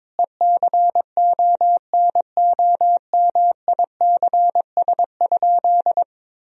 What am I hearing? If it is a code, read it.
Morse code, heard as a keyed tone, about 22 wpm, 700 Hz. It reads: ECONOMICS?
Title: Sentences from Top 1000 Words - Review Part 3 - 22wpm